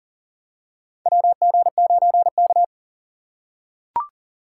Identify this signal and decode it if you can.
Morse code, heard as a keyed tone, about 40 wpm, 700 Hz. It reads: WG9K
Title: USA Call Signs - 2x Character Spacing - 40wpm